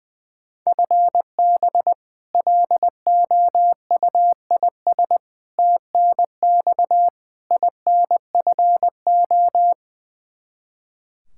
Morse code, heard as a keyed tone, about 20 wpm, 700 Hz. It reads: FB LOUIS TNX INFO
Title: Practice QSO - Exchange 3 - More Info - 20wpm